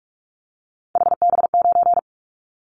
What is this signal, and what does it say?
Morse code, heard as a keyed tone, about 44 wpm, 700 Hz. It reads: HB9
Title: Callsign Prefixes for Top 25 DXCC - 45wpm